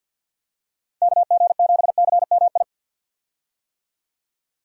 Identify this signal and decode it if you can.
Morse code, heard as a keyed tone, about 50 wpm, 700 Hz. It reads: KG7CMI
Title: USA Call Signs - 6 Characters - 50wpm